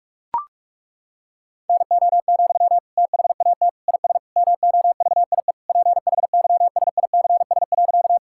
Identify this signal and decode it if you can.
Morse code, heard as a keyed tone, about 41 wpm, 700 Hz. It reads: NO, THAT IS MOVIE PHYSICS.
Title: Sentences from Top 1000 Words - Review Part 5 - 45wpm